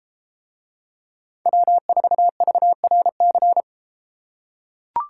Morse code, heard as a keyed tone, about 30 wpm, 700 Hz. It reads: W4VRC